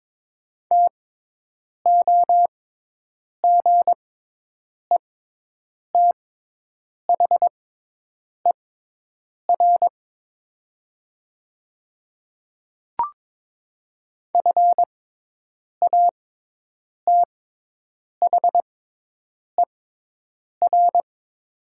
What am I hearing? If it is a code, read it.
Morse code, heard as a keyed tone, about 22 wpm, spaced as for 8 wpm, 700 Hz. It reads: TOGETHER FATHER
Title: 6x Character Spacing - Top 300 Words - 22wpm